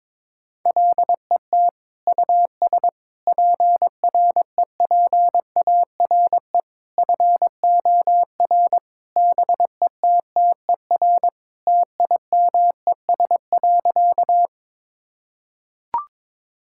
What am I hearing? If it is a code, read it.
Morse code, heard as a keyed tone, about 22 wpm, 700 Hz. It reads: LET US PREPARE FOR BETTER TIMES.
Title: Sentences from Top 1000 Words - Review Part 6 - 22wpm